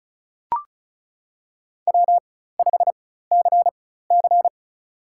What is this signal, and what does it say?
Morse code, heard as a keyed tone, about 34 wpm, spaced as for 16 wpm, 700 Hz. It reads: W5CC
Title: USA Call Signs - 4 Characters - 4x Character Spacing - 35wpm